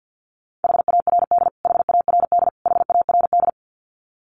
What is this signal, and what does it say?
Morse code, heard as a keyed tone, about 49 wpm, 700 Hz. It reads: HARD HARD HARD
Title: Top 300 Words Repeated 50wpm